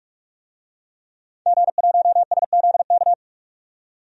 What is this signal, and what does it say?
Morse code, heard as a keyed tone, about 45 wpm, 700 Hz. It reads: G1SZK
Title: DXCC-100 Call Signs - 5 Characters - 45wpm